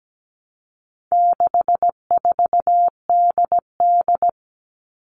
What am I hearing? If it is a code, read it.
Morse code, heard as a keyed tone, about 17 wpm, 700 Hz. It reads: 64DD
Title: All Cut Numbers - 4-Digit 17wpm